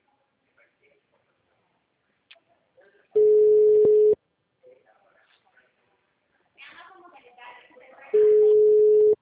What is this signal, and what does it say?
Ringback tone (Europe, 425 Hz), with speech.